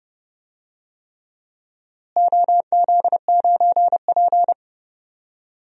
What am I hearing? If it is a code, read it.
Morse code, heard as a keyed tone, about 30 wpm, 700 Hz. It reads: OZ9P
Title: DXCC-100 Call Signs - 4 Characters - Repeated 3x - 30wpm